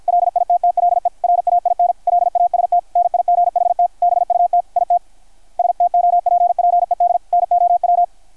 Morse code, heard as a keyed tone, about 51 wpm, 660 Hz, with radio noise. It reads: LITTLE RAIN LAST NIGHT BUT IT STOPPED NOW
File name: archived / CRSnet